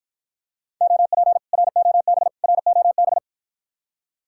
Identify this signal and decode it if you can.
Morse code, heard as a keyed tone, about 53 wpm, 700 Hz. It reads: OP ROB ROB